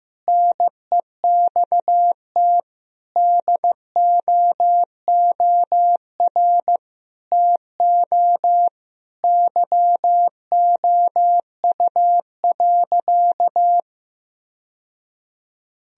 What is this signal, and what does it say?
Morse code, heard as a keyed tone, about 15 wpm, 700 Hz. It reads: NEXT DOOR TO YOU.
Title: Sentences from Top 1000 Words - Review Part 1 - 15wpm